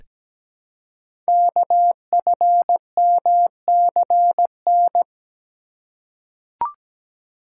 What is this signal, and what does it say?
Morse code, heard as a keyed tone, about 17 wpm, 700 Hz. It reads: KFMCN
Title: Code Groups - 5 Letters - 17wpm